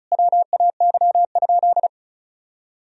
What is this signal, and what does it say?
Morse code, heard as a keyed tone, about 36 wpm, 700 Hz. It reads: WAY?
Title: Sentences from Top 1000 Words - Review Part 1 - 35wpm